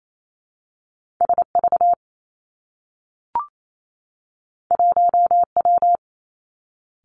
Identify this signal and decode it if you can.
Morse code, heard as a keyed tone, about 28 wpm, 700 Hz. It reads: S V 1 W
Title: Two Challenging Letter-Number Combinations 28wpm